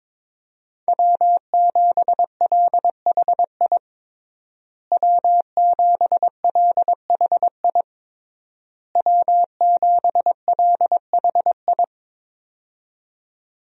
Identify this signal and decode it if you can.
Morse code, heard as a keyed tone, about 22 wpm, 700 Hz. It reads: W7LHI W7LHI W7LHI
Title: USA Call Signs - 5 Characters - Repeated 3x - 22wpm